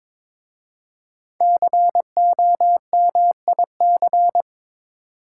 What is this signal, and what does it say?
Morse code, heard as a keyed tone, about 22 wpm, 700 Hz. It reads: COMIC
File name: 5 Letter Words - First 3 Letters Given - 22wpm